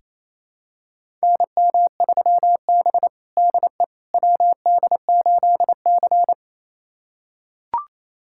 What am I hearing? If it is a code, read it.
Morse code, heard as a keyed tone, about 28 wpm, 700 Hz. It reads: NM3B DE WD8C
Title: USA Callsign DE Callsign - 4 Characters - 28wpm